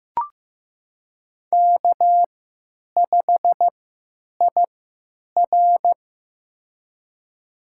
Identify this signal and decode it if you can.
Morse code, heard as a keyed tone, about 15 wpm, spaced as for 9 wpm, 700 Hz. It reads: K5IR
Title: USA Call Signs - 3x Character Spacing - 15wpm